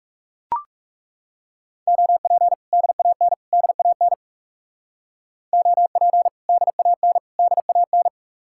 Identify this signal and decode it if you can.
Morse code, heard as a keyed tone, about 36 wpm, 700 Hz. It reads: OP DAN DAN OP DAN DAN
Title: Speed-Racing Common QSO Phrases - 30wpm